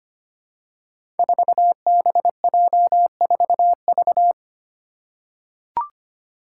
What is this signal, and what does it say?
Morse code, heard as a keyed tone, about 25 wpm, 700 Hz. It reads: 4BJ4V
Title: Five Challenging Letter-Number Combinations 25wpm